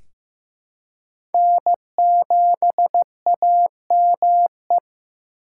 Morse code, heard as a keyed tone, about 15 wpm, 700 Hz. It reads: N7AME
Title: K1USN SST QSO Practice - 15wpm